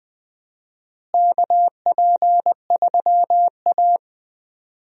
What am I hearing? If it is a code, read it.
Morse code, heard as a keyed tone, about 20 wpm, 700 Hz. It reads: KP3A
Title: USA Call Signs - 4 Characters - Repeated 2x - 20wpm